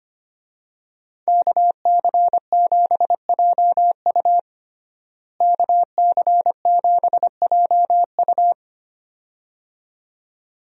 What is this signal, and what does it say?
Morse code, heard as a keyed tone, about 25 wpm, 700 Hz. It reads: KC7JU KC7JU
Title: USA Call Signs - 5 Characters - Repeated 2x - 25wpm